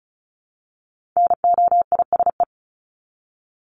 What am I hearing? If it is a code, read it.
Morse code, heard as a keyed tone, about 34 wpm, 700 Hz. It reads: NOISE
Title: Top 1501-1750 Words 35wpm